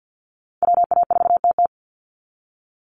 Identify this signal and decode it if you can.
Morse code, heard as a keyed tone, about 50 wpm, 700 Hz. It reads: WA4TT
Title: Call Signs Repeated 2x - 50wpm